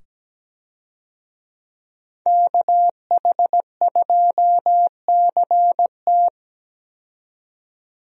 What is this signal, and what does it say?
Morse code, heard as a keyed tone, about 17 wpm, 700 Hz. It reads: KH2CT
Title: USA Call Signs - 5 Characters - 17wpm